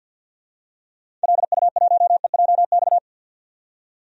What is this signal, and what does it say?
Morse code, heard as a keyed tone, about 50 wpm, 700 Hz. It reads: LU1EJX
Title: DXCC-100 Call Signs - 6 Characters - 50wpm